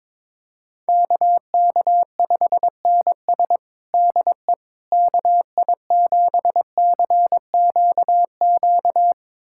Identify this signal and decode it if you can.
Morse code, heard as a keyed tone, about 22 wpm, 700 Hz. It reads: KK5NS DE KI7CQQ
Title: USA Callsign DE Callsign 22wpm